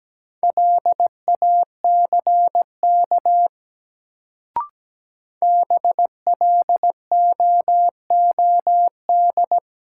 Morse code, heard as a keyed tone, about 17 wpm, 700 Hz. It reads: LACK BLOOD
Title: Sentences from Top 1100 Words - New Word Drill Alt - 17wpm